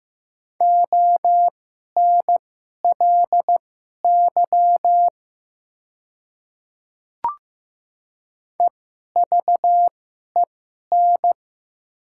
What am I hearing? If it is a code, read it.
Morse code, heard as a keyed tone, about 15 wpm, spaced as for 11 wpm, 700 Hz. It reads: ONLY EVEN
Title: 2x Character Spacing - Top 200 Words - 15wpm